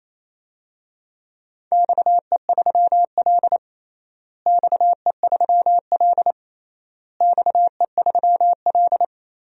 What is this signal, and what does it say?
Morse code, heard as a keyed tone, about 28 wpm, 700 Hz. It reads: XE3L XE3L XE3L
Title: DXCC-100 Call Signs - 4 Characters - Repeated 3x - 28wpm